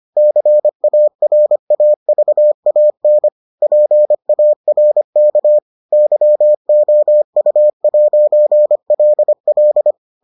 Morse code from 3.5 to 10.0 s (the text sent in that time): PARK YOU'LL